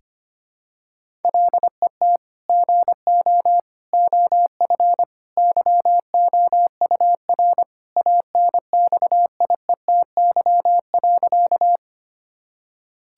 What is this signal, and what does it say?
Morse code, heard as a keyed tone, about 25 wpm, 700 Hz. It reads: LET GO OF YOUR ANXIETY.
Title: Sentences from Top 1600 Words 25wpm